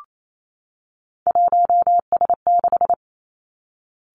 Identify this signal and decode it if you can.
Morse code, heard as a keyed tone, about 28 wpm, 690 Hz. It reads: 1S6
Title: Three Challenging Letter-Number Combinations 28wpm